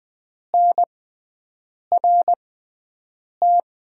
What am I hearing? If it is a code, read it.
Morse code, heard as a keyed tone, about 20 wpm, spaced as for 7 wpm, 700 Hz. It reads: NRT
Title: USA Call Signs - 6x Character Spacing - 20wpm